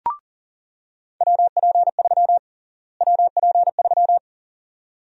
Morse code, heard as a keyed tone, about 37 wpm, 700 Hz. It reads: WP3 WP3
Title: USA Call Sign Prefixes - Repeated 2x - 40wpm